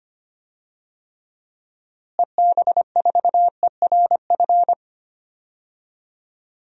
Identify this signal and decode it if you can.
Morse code, heard as a keyed tone, about 25 wpm, 700 Hz. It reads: EB4ERF
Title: DXCC-100 Call Signs - 6 Characters - Repeated 2x - 25wpm